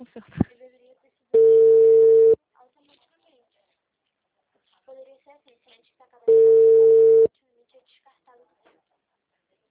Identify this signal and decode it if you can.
Ringback tone (China, 450 Hz), with speech.